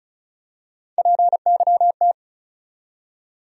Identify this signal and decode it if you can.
Morse code, heard as a keyed tone, about 36 wpm, 700 Hz. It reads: PYT
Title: USA Call Sign Postfixes - 3 Letters 35wpm